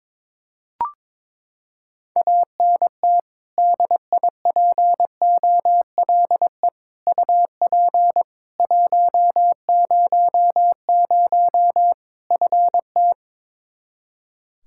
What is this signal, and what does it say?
Morse code, heard as a keyed tone, about 22 wpm, 700 Hz. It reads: ANT DIPOLE UP 100 FT
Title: Practice QSO - Exchange 2 - Info - 22wpm